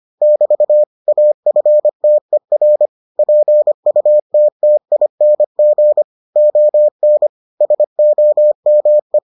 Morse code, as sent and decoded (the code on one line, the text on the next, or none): -...- / .- ..-. - . .-. / .--. ..- - - .. -. --. / --- -. / ... --- -- .
<BT> AFTER PUTTING ON SOME